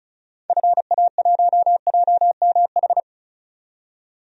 Morse code, heard as a keyed tone, about 34 wpm, 700 Hz. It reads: FA1JMH
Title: DXCC-100 Call Signs - 6 Characters - 35wpm